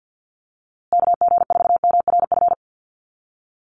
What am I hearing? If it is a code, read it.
Morse code, heard as a keyed tone, about 50 wpm, 700 Hz. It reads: KG4MRF
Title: USA Call Signs Repeated 2x 50wpm